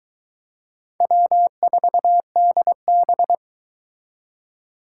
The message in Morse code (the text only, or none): W4DB